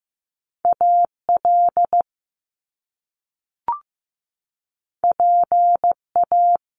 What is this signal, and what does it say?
Morse code, heard as a keyed tone, about 15 wpm, 700 Hz. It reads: AL PA
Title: Top Two Letter Combinations in Words 15wpm